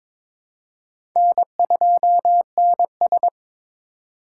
Morse code, heard as a keyed tone, about 21 wpm, 700 Hz. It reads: N2NS
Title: K1USN SST QSO Practice - 22wpm